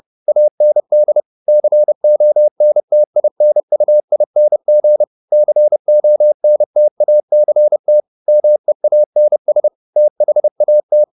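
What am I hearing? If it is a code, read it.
Morse code, heard as a keyed tone, about 30 wpm, 600 Hz. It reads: AND CONTINUING CONTACT MEANS THAT
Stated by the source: Morse code at 30 wpm, the novel Never by Ken Follett in English, from CW med Gnister.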